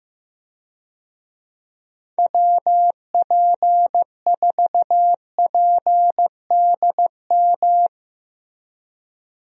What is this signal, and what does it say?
Morse code, heard as a keyed tone, about 16 wpm, 700 Hz. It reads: WP4PDM
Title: USA Call Signs - 6 Characters - Repeated 3x - 15wpm